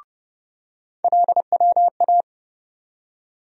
Morse code, heard as a keyed tone, about 30 wpm, 700 Hz. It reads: LWA
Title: USA Call Sign Postfixes - 3 Letters 30wpm